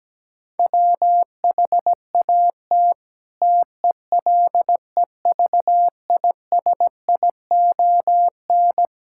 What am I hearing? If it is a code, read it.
Morse code, heard as a keyed tone, about 17 wpm, 700 Hz. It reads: WHAT TELEVISION
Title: Sentences from Top 1000 Words - Review Part 5 - 17wpm